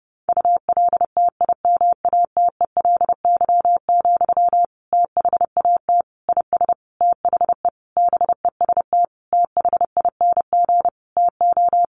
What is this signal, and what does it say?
Morse code, heard as a keyed tone, about 30 wpm, 700 Hz. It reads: ULTIMATELY, THAT IS THE BEST THING TO
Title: Sentences from Top 1200 Words 30wpm